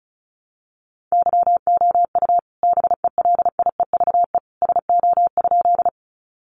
Morse code, heard as a keyed tone, about 34 wpm, 700 Hz. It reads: YOU BELIEVE SO?
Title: Sentences from Top 1000 Words 35wpm